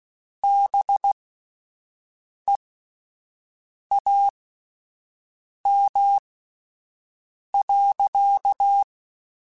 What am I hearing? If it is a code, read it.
Morse code, heard as a keyed tone, about 16 wpm, spaced as for 5 wpm, 800 Hz. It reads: BEAM.
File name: (spaced) / QSOs Spaced-mp3s / WD0BGP 1 @10 sp